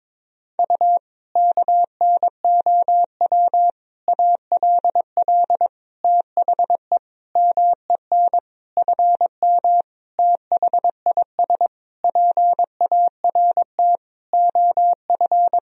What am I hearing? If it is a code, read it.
Morse code, heard as a keyed tone, about 22 wpm, 700 Hz. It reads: U KNOW ALL THE MEN FM THIS PART OF
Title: Sentences from Top 300 Words and Abbreviations 22wpm